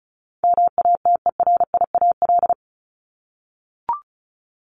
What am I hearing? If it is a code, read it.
Morse code, heard as a keyed tone, about 34 wpm, 700 Hz. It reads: MATERIAL
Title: ES:S Suffix - Top 100 Words with Base Words 35wpm